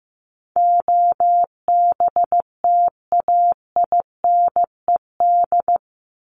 Morse code, heard as a keyed tone, about 15 wpm, 700 Hz. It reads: OBTAINED
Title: ED Suffix - Top 100 Words with Base Words 15wpm